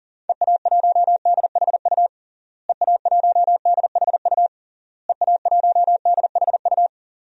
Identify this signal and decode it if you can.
Morse code, heard as a keyed tone, about 40 wpm, 700 Hz. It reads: EA1DHU EA1DHU EA1DHU
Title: DXCC-100 Call Signs - 6 Characters - Repeated 3x - 40wpm